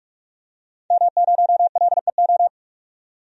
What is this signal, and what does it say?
Morse code, heard as a keyed tone, about 46 wpm, 700 Hz. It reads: M0LEO
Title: DXCC-100 Call Signs - 5 Characters - 45wpm